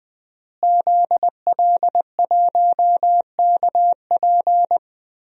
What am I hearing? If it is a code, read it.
Morse code, heard as a keyed tone, about 20 wpm, 700 Hz. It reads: ZL1KP
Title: DXCC-100 Call Signs - 5 Characters - Repeated 3x - 20wpm